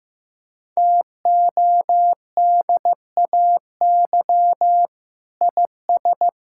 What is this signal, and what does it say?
Morse code, heard as a keyed tone, about 15 wpm, 700 Hz. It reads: TODAY IS
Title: Sentences from Top 1600 Words 15wpm